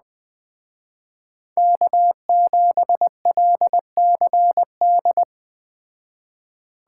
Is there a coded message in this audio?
Yes: Morse code from 1.6 to 5.2 s, heard as a keyed tone, about 21 wpm, 700 Hz. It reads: K7LCD